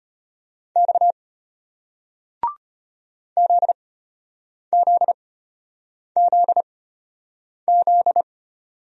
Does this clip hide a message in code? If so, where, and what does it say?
Yes: Morse code from 0.8 to 8.2 s, heard as a keyed tone, about 33 wpm, spaced as for 8 wpm, 700 Hz. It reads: X ZZZZ